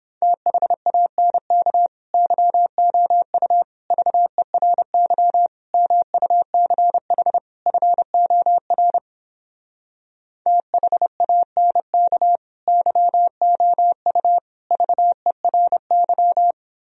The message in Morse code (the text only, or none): THANK YOU VERY MUCH FOR THANK YOU VERY